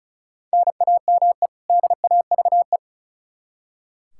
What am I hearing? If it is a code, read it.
Morse code, heard as a keyed tone, about 36 wpm, 700 Hz. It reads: NAME DAVE
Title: Practice QSO - Exchange 1 - Beginning - 35wpm